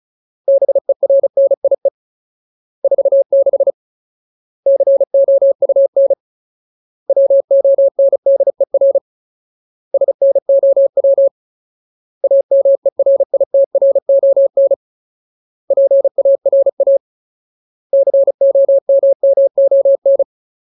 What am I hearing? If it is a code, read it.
Morse code, heard as a keyed tone, about 34 wpm, 550 Hz. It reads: BERNIE 46 COUN WONDER SNOW AMERITRON PARA COMMON
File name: Ultimate Word List / 35WPM Ultimate